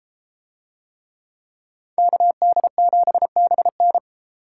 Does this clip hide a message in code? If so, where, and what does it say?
Yes: Morse code from 2.0 to 4.0 s, heard as a keyed tone, about 30 wpm, 700 Hz. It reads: KD7BN